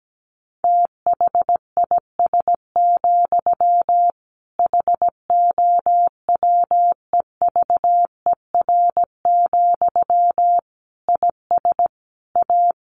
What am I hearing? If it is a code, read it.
Morse code, heard as a keyed tone, about 17 wpm, 700 Hz. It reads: THIS, HOWEVER, IS A
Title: Sentences from Top 1500 Words 17wpm